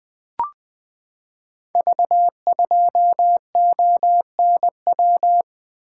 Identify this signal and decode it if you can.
Morse code, heard as a keyed tone, about 20 wpm, 700 Hz. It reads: V2ONW
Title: Code Groups - 5 Letter:Numbers - 20wpm